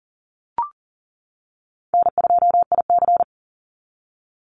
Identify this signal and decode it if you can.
Morse code, heard as a keyed tone, about 40 wpm, 700 Hz. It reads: N2IC
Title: Easy Call Signs - 40wpm